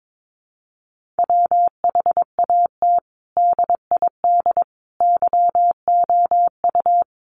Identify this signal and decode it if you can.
Morse code, heard as a keyed tone, about 22 wpm, 700 Hz. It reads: WHAT DID YOU
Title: Sentences from Top 1500 Words 22wpm